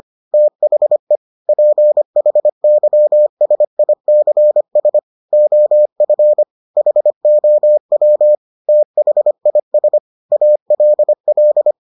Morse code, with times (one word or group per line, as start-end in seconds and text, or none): 0.3-1.2 THE
1.5-5.0 PHYSICS
5.3-6.4 OF
6.8-8.3 HOW
8.7-10.0 THIS
10.3-11.7 ALL